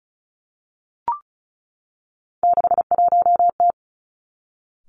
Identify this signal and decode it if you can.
Morse code, heard as a keyed tone, about 36 wpm, 700 Hz. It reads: 61T
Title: Three Challenging Letter-Number Combinations 35wpm